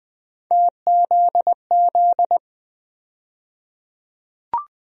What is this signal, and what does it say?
Morse code, heard as a keyed tone, about 20 wpm, 700 Hz. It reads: TZZ
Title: USA Call Sign Postfixes - 3 Letters 20wpm